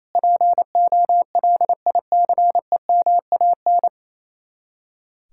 Morse code, heard as a keyed tone, about 36 wpm, 700 Hz. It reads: POLICEMAN